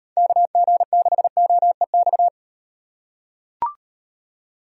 Morse code, heard as a keyed tone, about 38 wpm, 700 Hz. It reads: KG6OEX